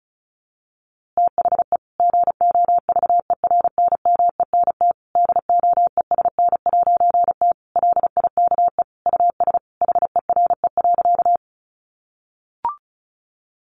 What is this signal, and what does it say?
Morse code, heard as a keyed tone, about 34 wpm, 700 Hz. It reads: THE GOVERNMENT DOESN'T LIKE US HERE.
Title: A Sound Of Thunder 35wpm